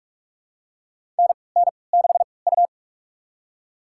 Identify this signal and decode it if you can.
Morse code, heard as a keyed tone, about 45 wpm, spaced as for 25 wpm, 700 Hz. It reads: NN6U
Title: USA Call Signs - 4 Characters - 3x Character Spacing - 45wpm